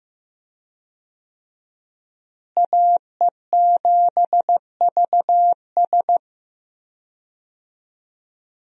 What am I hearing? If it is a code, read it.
Morse code, heard as a keyed tone, about 15 wpm, 700 Hz. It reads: AE7VS
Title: USA Call Signs - 5 Characters - Repeated 2x - 15wpm